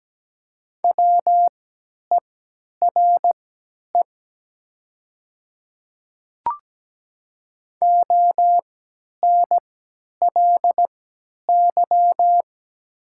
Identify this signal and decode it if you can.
Morse code, heard as a keyed tone, about 17 wpm, spaced as for 10 wpm, 700 Hz. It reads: WERE ONLY
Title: 3x Character Spacing - Top 500 Words - 17wpm